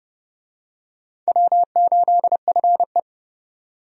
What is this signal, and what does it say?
Morse code, heard as a keyed tone, about 30 wpm, 700 Hz. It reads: W8FE
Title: USA Call Signs - 4 Characters - 30wpm